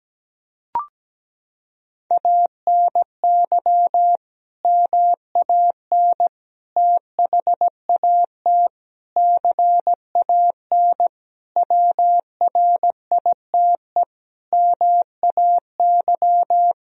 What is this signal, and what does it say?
Morse code, heard as a keyed tone, about 17 wpm, 700 Hz. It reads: ANY MAN THAT CAN WRITE MAY
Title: Sentences from Top 1000 Words - Review Part 1 - 17wpm